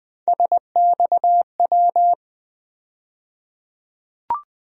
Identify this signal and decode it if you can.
Morse code, heard as a keyed tone, about 21 wpm, 700 Hz. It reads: SXW
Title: USA Call Sign Postfixes - 3 Letters 20wpm